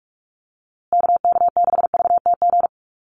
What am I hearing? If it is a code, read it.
Morse code, heard as a keyed tone, about 44 wpm, 700 Hz. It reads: KK6VTG
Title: USA Call Signs Repeated 2x 45wpm